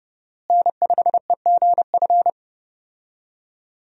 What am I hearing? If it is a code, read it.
Morse code, heard as a keyed tone, about 30 wpm, 700 Hz. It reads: N5EGF
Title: USA Call Signs - 5 Characters - Repeated 2x - 30wpm